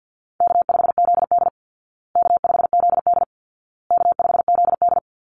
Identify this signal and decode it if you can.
Morse code, heard as a keyed tone, about 48 wpm, 700 Hz. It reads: K5ZD K5ZD K5ZD
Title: Easy Call Signs Repeated 3x - 50wpm